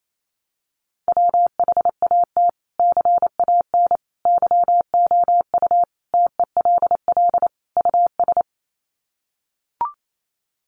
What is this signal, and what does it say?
Morse code, heard as a keyed tone, about 28 wpm, 700 Hz. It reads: WHAT CAN YOU TELL US
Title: Sets of 5 Words Encore 28wpm